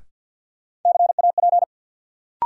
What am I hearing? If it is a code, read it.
Morse code, heard as a keyed tone, about 52 wpm, 700 Hz. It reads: CAP
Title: Sentences from Top 1600 Words - New Word Drill Alt - 50wpm